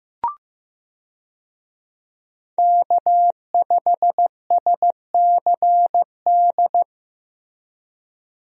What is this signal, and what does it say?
Morse code, heard as a keyed tone, about 16 wpm, 700 Hz. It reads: K5SCD
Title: USA Call Signs - 5 Characters - Repeated 2x - 15wpm